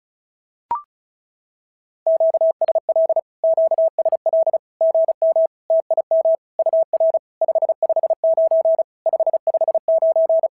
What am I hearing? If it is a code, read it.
Morse code, heard as a keyed tone, about 36 wpm, 650 Hz. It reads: QSL QSL GM TIM UR 559 559
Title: SOTA QSO Practice 35wpm